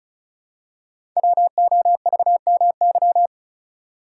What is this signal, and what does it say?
Morse code, heard as a keyed tone, about 36 wpm, 690 Hz. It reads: WOVMY
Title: Code Groups - 5 Letters - 35wpm